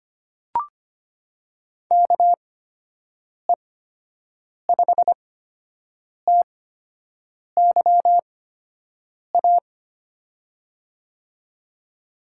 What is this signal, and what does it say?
Morse code, heard as a keyed tone, about 25 wpm, spaced as for 7 wpm, 700 Hz. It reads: KE5TYA
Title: USA Call Signs - 6 Characters - 8x Character Spacing - 25wpm